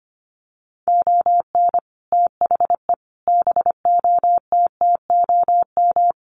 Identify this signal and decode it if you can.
Morse code, heard as a keyed tone, about 25 wpm, 700 Hz. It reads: ON THE BOTTOM
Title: Sets of 5 Words Encore2 25wpm